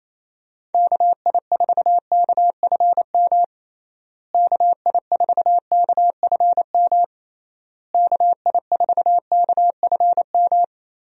Morse code, heard as a keyed tone, about 28 wpm, 700 Hz. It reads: KI4KFM KI4KFM KI4KFM
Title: USA Call Signs - 6 Characters - Repeated 3x - 28wpm